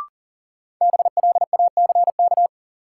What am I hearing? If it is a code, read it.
Morse code, heard as a keyed tone, about 40 wpm, 700 Hz. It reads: BLACK
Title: Sentences from Top 400 Words - New Word Drill - 40wpm